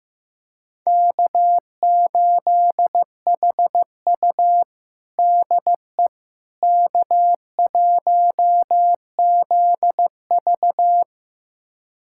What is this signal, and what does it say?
Morse code, heard as a keyed tone, about 15 wpm, 700 Hz. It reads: K8HU DE K1ZV
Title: USA Callsign DE Callsign - 4 Characters - 15wpm